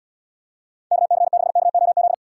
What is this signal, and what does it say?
Morse code, heard as a keyed tone, about 77 wpm, 700 Hz. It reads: KC6KOZ